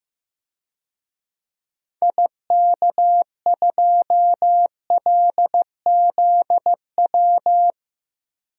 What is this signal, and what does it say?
Morse code, heard as a keyed tone, about 16 wpm, 700 Hz. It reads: IK2LZW
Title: DXCC-100 Call Signs - 6 Characters - Repeated 2x - 15wpm